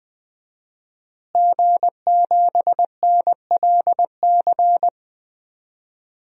Morse code, heard as a keyed tone, about 21 wpm, 700 Hz. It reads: G7NLC
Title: DXCC-100 Call Signs - 5 Characters - Repeated 2x - 20wpm